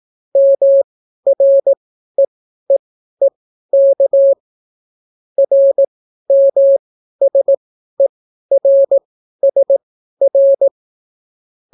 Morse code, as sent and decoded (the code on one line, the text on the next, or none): -- .-. . -.- / .-. -- ... .-. ... .-.
MREEEK RMSERSR